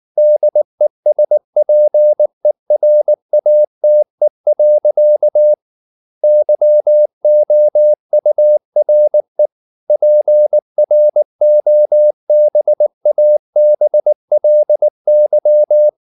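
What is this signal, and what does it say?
Morse code, heard as a keyed tone, about 19 wpm, 600 Hz. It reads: DESPERATE. YOURE PROBABLY
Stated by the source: Morse code at 19 wpm from CW med Gnister, reading Ken Follett's novel Never in English.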